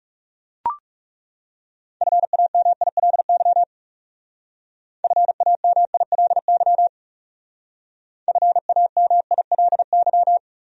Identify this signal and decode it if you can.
Morse code, heard as a keyed tone, about 39 wpm, 700 Hz. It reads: FAMILY FAMILY FAMILY